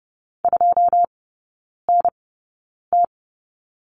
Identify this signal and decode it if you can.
Morse code, heard as a keyed tone, about 30 wpm, spaced as for 9 wpm, 700 Hz. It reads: 2NT
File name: 081 Lesson 14e - Number 2 - Three Characters 30wpm